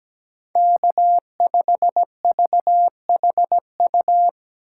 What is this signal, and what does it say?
Morse code, heard as a keyed tone, about 17 wpm, 700 Hz. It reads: K5VHU
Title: USA Call Signs - 5 Characters - 4x Character Spacing - 17wpm